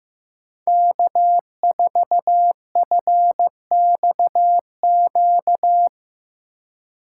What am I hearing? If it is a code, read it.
Morse code, heard as a keyed tone, about 16 wpm, 700 Hz. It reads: K4FXQ
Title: Speed Racing - USA 1x3 Call Signs 15wpm